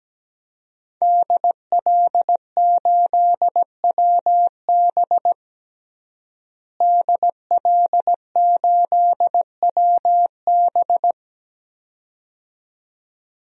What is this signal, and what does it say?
Morse code, heard as a keyed tone, about 17 wpm, 700 Hz. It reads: DL8WB DL8WB
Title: DXCC-100 Call Signs - 5 Characters - Repeated 2x - 17wpm